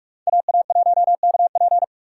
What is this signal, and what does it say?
Morse code, heard as a keyed tone, about 46 wpm, 700 Hz. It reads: AA1KP
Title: USA Call Signs - 5 Characters - 6x Character Spacing - 45wpm